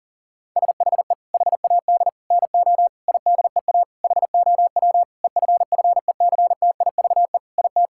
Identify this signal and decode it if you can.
Morse code, heard as a keyed tone, about 40 wpm, 700 Hz. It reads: SHE HAD NO IDEA HOW EFFECTIVE IT
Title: Sentences from Top 1000 Words - Review Part 6 - 40wpm